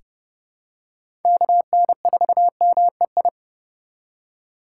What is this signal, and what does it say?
Morse code, heard as a keyed tone, about 30 wpm, 700 Hz. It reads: KN4MEI
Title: USA Call Signs - 6 Characters - 3x Character Spacing - 30wpm